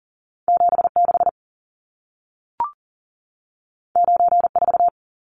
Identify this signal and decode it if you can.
Morse code, heard as a keyed tone, about 40 wpm, 700 Hz. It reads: 76 94
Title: Numbers - 2-Digit 40wpm